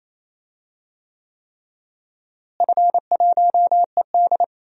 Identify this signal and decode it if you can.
Morse code, heard as a keyed tone, about 28 wpm, 700 Hz. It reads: F1ED
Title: DXCC-100 Call Signs - 4 Characters - Repeated 3x - 28wpm